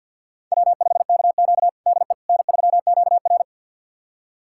Morse code, heard as a keyed tone, about 51 wpm, 700 Hz. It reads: W5KQ DE N3QR
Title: USA Callsign DE Callsign - 4 Characters - 50wpm